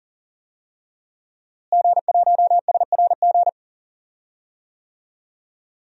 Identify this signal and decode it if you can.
Morse code, heard as a keyed tone, about 38 wpm, 700 Hz. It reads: G1SRG
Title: DXCC-100 Call Signs - 5 Characters - Repeated 2x - 40wpm